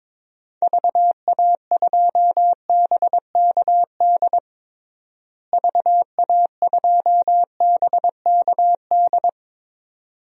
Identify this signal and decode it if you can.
Morse code, heard as a keyed tone, about 22 wpm, 700 Hz. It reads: VA2BKD VA2BKD
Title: DXCC-100 Call Signs - 6 Characters - Repeated 2x - 22wpm